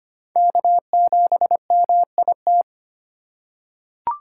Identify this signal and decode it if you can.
Morse code, heard as a keyed tone, about 25 wpm, 700 Hz. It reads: K7MIT
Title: USA Call Signs - 5 Characters - 3x Character Spacing - 25wpm